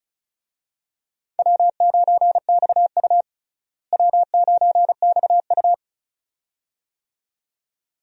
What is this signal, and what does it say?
Morse code, heard as a keyed tone, about 36 wpm, 700 Hz. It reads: W9XU W9XU
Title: USA Call Signs - 4 Characters - Repeated 2x - 35wpm